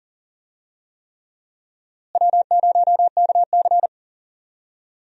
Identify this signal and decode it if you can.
Morse code, heard as a keyed tone, about 39 wpm, 700 Hz. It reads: W0KC
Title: USA Call Signs - 4 Characters - Repeated 3x - 40wpm